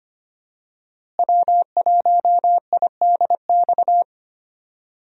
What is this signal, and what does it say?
Morse code, heard as a keyed tone, about 25 wpm, 700 Hz. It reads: W1IDX
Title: USA Call Signs - 5 Characters - Repeated 2x - 25wpm